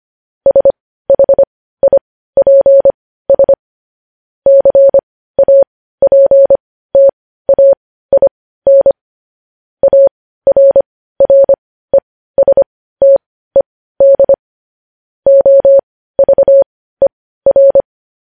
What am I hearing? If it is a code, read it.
Morse code, heard as a keyed tone, about 25 wpm, spaced as for 15 wpm, 550 Hz. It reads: SHIPS CAPTAIN ARRESTED OVER
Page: News Headlines in Morse Code at 15 WPM